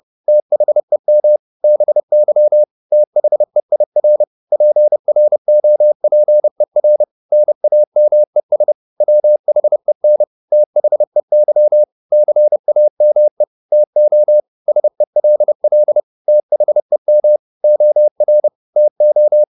Morse code, heard as a keyed tone, about 30 wpm, 600 Hz. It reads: THEM BY THEIR PROPER NAMES WHEN THEY CAME TO SELL THEM OR TO
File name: the old man and the sea / the old man and the sea 1 30wpm